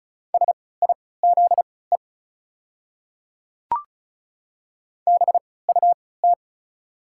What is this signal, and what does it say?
Morse code, heard as a keyed tone, about 36 wpm, spaced as for 20 wpm, 700 Hz. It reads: SIZE BUT
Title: 3x Character Spacing - Top 500 Words - 35wpm